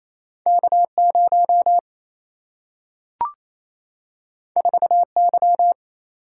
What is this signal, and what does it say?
Morse code, heard as a keyed tone, about 28 wpm, 700 Hz. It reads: K0 4Y
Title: Single-Letter Single-Number 28wpm